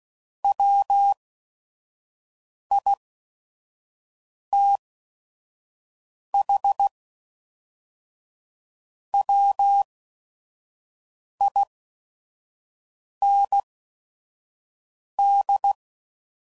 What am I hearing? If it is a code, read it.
Morse code, heard as a keyed tone, about 16 wpm, spaced as for 5 wpm, 800 Hz. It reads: WITH WIND
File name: CW Code Groups @5WPM / Weather Sets / Weather 5 5